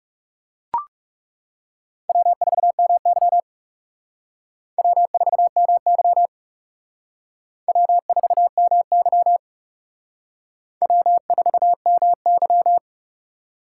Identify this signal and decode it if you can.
Morse code, heard as a keyed tone, about 37 wpm, 700 Hz. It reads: W4MY W4MY W4MY W4MY
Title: Speed Racing - USA 1x2 Call Signs 30wpm